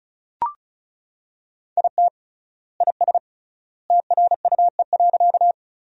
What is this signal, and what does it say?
Morse code, heard as a keyed tone, about 36 wpm, 700 Hz. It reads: IT IS TRUE.
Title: 3x Word-Spacing - Sentences from Top 400 Words - 35wpm